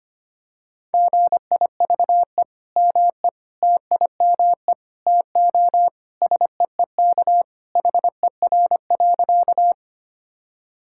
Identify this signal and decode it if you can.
Morse code, heard as a keyed tone, about 25 wpm, 700 Hz. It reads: GIVE ME TIME TO SEEK HER.
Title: Sentences from Top 1600 Words 25wpm